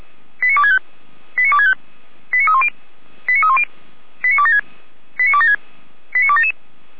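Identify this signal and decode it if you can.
Five-tone selective call (ZVEI), heard as quick runs of five tones, tones of about 80 ms. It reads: 89265 89265 89310 89310 89276 89276 89277
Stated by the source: ZVEI five-tone selective call received on an SDR.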